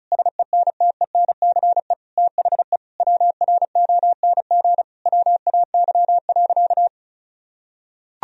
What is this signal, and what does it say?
Morse code, heard as a keyed tone, about 34 wpm, 700 Hz. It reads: SENTENCE THE WRONG WAY.
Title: Sentences from Top 1000 Words - Review Part 1 - 35wpm